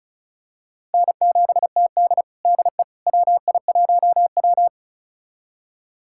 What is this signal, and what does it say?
Morse code, heard as a keyed tone, about 36 wpm, 700 Hz. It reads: N7TD DE WI1W
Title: USA Callsign DE Callsign - 4 Characters - 35wpm